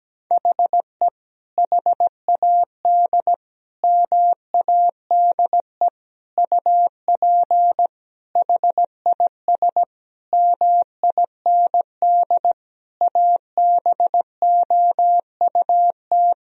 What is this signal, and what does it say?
Morse code, heard as a keyed tone, about 17 wpm, 700 Hz. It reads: HE HAD MADE UP HIS MIND ABOUT
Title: Sentences from Top 1000 Words - Review Part 3 - 17wpm